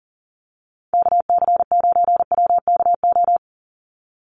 Morse code, heard as a keyed tone, about 40 wpm, 700 Hz. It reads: KC9WKO
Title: USA Call Signs 40wpm